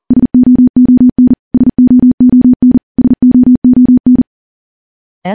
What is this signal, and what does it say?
Morse code, heard as a keyed tone, about 40 wpm, 250 Hz. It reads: SOON SOON SOON
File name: morse / morse-training-40WPM